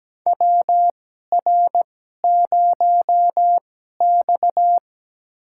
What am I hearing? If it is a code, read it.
Morse code, heard as a keyed tone, about 17 wpm, spaced as for 12 wpm, 700 Hz. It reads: WR0X
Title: USA Call Signs - 4 Characters - 2x Character Spacing - 17wpm